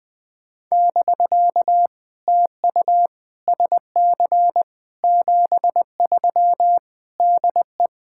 Morse code, heard as a keyed tone, about 20 wpm, 700 Hz. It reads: <BK> TU SC 73 DE
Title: POTA QSO Practice - 20wpm